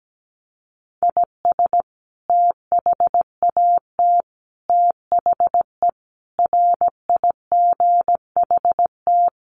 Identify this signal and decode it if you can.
Morse code, heard as a keyed tone, about 17 wpm, 700 Hz. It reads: IS THAT THE RIGHT
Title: Sentences from Top 1200 Words 17wpm